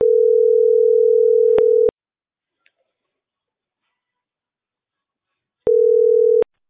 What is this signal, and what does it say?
Ringback tone (North America), 440 + 480 Hz.